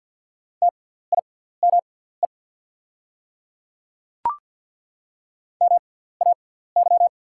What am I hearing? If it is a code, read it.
Morse code, heard as a keyed tone, about 48 wpm, spaced as for 17 wpm, 690 Hz. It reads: TIME MAY
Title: 6x Character Spacing - Top 200 Words - 50wpm